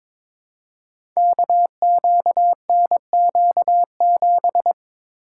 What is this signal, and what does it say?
Morse code, heard as a keyed tone, about 23 wpm, 700 Hz. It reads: KQNQ7
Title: Code Groups - 5 Letter:Numbers - 22wpm